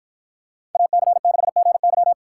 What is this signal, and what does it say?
Morse code, heard as a keyed tone, about 51 wpm, 700 Hz. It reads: AC6GY